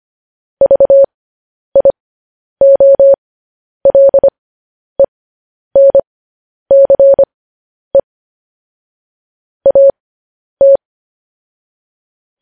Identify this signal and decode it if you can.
Morse code, heard as a keyed tone, about 25 wpm, spaced as for 10 wpm, 550 Hz. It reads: VIOLENCE AT